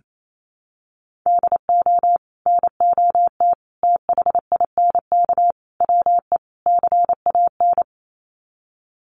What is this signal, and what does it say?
Morse code, heard as a keyed tone, about 28 wpm, 700 Hz. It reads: DO NOT THINK WE CAN
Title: Sets of 5 Words Encore2 28wpm